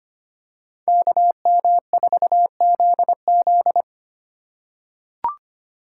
Morse code, heard as a keyed tone, about 25 wpm, 700 Hz. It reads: KM4ZZ
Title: K1USN SST QSO Practice - 25wpm